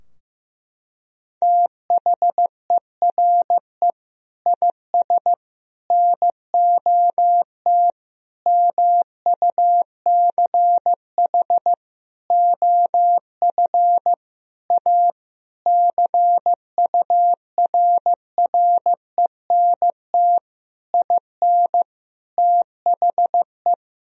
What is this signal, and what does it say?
Morse code, heard as a keyed tone, about 16 wpm, 700 Hz. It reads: THERE IS NOT MUCH OF A CURRENT IN THE
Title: Sentences from Top 1000 Words - Review Part 4 - 15wpm